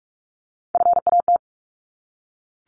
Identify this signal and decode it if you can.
Morse code, heard as a keyed tone, about 44 wpm, 700 Hz. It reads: FAT
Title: Top 501-750 Words 45wpm